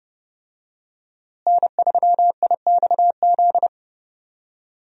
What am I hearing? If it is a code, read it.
Morse code, heard as a keyed tone, about 30 wpm, 700 Hz. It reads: N3IXZ
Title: USA Call Signs - 5 Characters - Repeated 2x - 30wpm